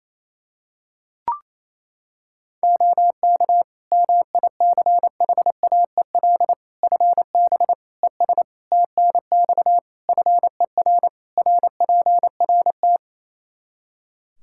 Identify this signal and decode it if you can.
Morse code, heard as a keyed tone, about 28 wpm, 700 Hz. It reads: OK MICHAEL FB ES TNX FER RPRT
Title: K1USN SST QSO Practice 28wpm